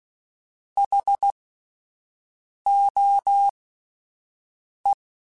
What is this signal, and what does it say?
Morse code, heard as a keyed tone, about 16 wpm, spaced as for 5 wpm, 800 Hz. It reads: HOE
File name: Alphabetical / Alphabetical mp3s / Hs @5